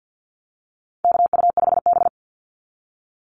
Morse code, heard as a keyed tone, about 50 wpm, 700 Hz. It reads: KU5B
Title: Call Signs Repeated 2x - 50wpm